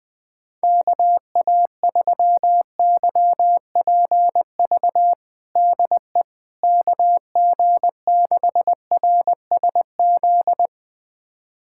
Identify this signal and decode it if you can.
Morse code, heard as a keyed tone, about 20 wpm, 700 Hz. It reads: KA3YPV DE KG6RSZ
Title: USA Callsign DE Callsign 20wpm